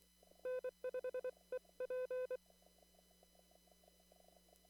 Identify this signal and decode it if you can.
Morse code, heard as a keyed tone, about 23 wpm, 510 Hz. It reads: N5EP
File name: CW / kt5j wpx cw-op-k5pi